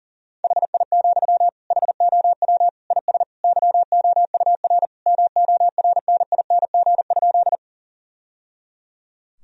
Morse code, heard as a keyed tone, about 40 wpm, 700 Hz. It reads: HI, HOW IS YOUR MORNING?
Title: Sentences from Top 1000 Words - Review Part 3 - 40wpm